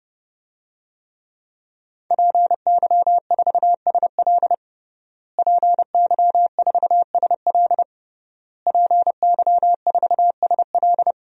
Morse code, heard as a keyed tone, about 30 wpm, 700 Hz. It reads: PY4SL PY4SL PY4SL
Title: DXCC-100 Call Signs - 5 Characters - Repeated 3x - 30wpm